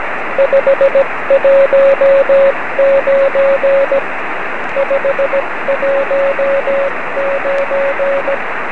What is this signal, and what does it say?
Morse code, heard as a keyed tone, about 17 wpm, 550 Hz, with radio noise. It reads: 519 519
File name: 9cm